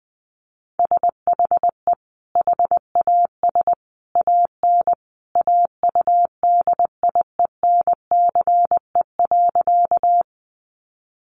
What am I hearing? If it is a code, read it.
Morse code, heard as a keyed tone, about 20 wpm, 700 Hz. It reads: SHE HAS AN AUDIENCE.
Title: Sentences from Top 800 Words 20wpm